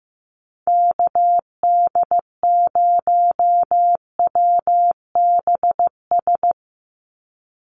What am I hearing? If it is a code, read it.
Morse code, heard as a keyed tone, about 15 wpm, 700 Hz. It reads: KD0WBS
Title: USA Call Signs 15wpm